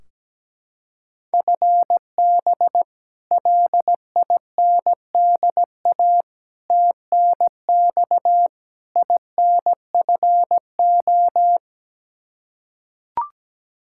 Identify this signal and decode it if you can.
Morse code, heard as a keyed tone, about 17 wpm, 700 Hz. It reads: FB LINDA TNX INFO
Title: Practice QSO - Exchange 3 - More Info - 17wpm